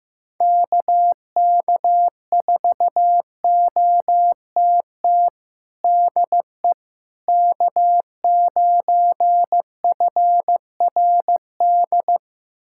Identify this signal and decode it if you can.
Morse code, heard as a keyed tone, about 16 wpm, 700 Hz. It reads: KK4OTT DE K9FRD
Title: USA Callsign DE Callsign 15wpm